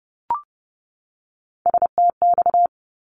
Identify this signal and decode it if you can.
Morse code, heard as a keyed tone, about 30 wpm, 700 Hz. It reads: STX
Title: ARRL Sections 30wpm